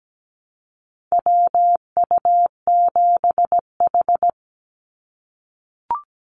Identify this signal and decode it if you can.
Morse code, heard as a keyed tone, about 17 wpm, 700 Hz. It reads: WU7H
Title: NA SOTA Activator Callsigns 17wpm